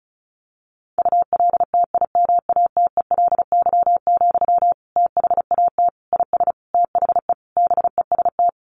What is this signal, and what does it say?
Morse code, heard as a keyed tone, about 34 wpm, 700 Hz. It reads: ULTIMATELY, THAT IS THE BEST
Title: Sentences from Top 1200 Words 35wpm